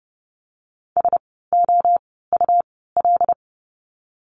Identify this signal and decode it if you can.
Morse code, heard as a keyed tone, about 30 wpm, spaced as for 17 wpm, 700 Hz. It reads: SOUL
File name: Lesson 17f - Letter u - Words 30wpm